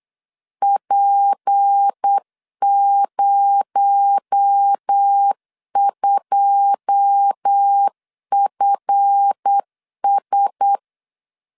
Morse code, heard as a keyed tone, about 8 wpm, 800 Hz. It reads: P02FS
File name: LEZIONI / 9 2 8wpm 3437